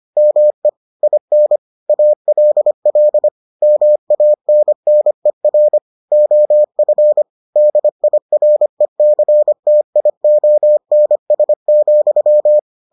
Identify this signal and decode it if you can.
Morse code, heard as a keyed tone, about 25 wpm, 600 Hz. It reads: ME IN ALL MANNER OF DIRECTIONS,